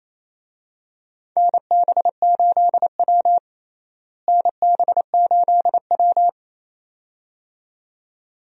Morse code, heard as a keyed tone, about 28 wpm, 700 Hz. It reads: NB8W NB8W
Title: USA Call Signs - 4 Characters - Repeated 2x - 28wpm